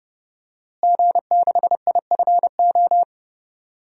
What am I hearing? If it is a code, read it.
Morse code, heard as a keyed tone, about 31 wpm, 700 Hz. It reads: G6IFO
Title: DXCC-100 Call Signs - 5 Characters - Repeated 3x - 30wpm